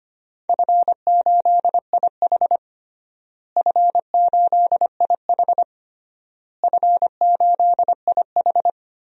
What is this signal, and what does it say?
Morse code, heard as a keyed tone, about 25 wpm, 700 Hz. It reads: F8IH F8IH F8IH
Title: DXCC-100 Call Signs - 4 Characters - Repeated 3x - 25wpm